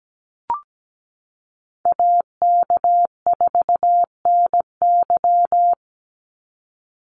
Morse code, heard as a keyed tone, about 17 wpm, 700 Hz. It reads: AK4NY
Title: Call Signs - 17wpm